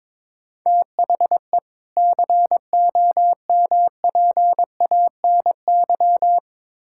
Morse code, heard as a keyed tone, about 22 wpm, 700 Hz. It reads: THE COMPANY
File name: Build A Phrase - 2-Word Phrases - 22wpm